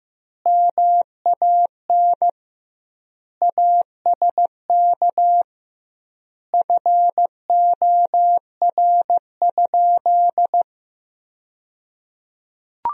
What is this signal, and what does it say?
Morse code, heard as a keyed tone, about 15 wpm, 700 Hz. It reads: MAN ASK FOR?
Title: 2x Word-Spacing - Sentences from Top 200 Words - 15wpm